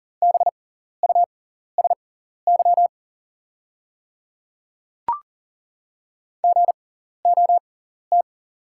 Morse code, heard as a keyed tone, about 39 wpm, spaced as for 14 wpm, 690 Hz. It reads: BUSY GOT
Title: 6x Character Spacing - Top 500 Words - 40wpm